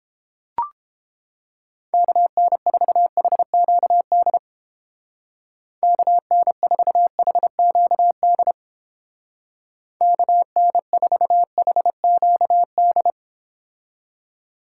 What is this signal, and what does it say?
Morse code, heard as a keyed tone, about 29 wpm, 700 Hz. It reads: KN4HQD KN4HQD KN4HQD